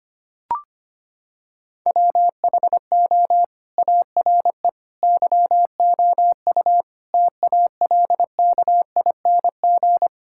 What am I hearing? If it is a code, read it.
Morse code, heard as a keyed tone, about 25 wpm, 700 Hz. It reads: WHO ARE YOU TALKING
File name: practice / 25wpm / Sentences with 2-Word Phrases - 25wpm